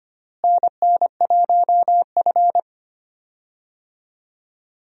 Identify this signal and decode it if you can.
Morse code, heard as a keyed tone, about 25 wpm, 700 Hz. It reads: NN1F
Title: USA Call Signs - 4 Characters - Repeated 2x - 25wpm